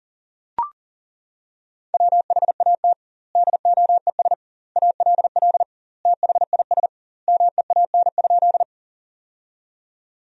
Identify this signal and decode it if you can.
Morse code, heard as a keyed tone, about 40 wpm, 700 Hz. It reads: WHAT DOES ALL THIS MEAN?
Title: 2x Word-Spacing - Sentences from Top 200 Words - 40wpm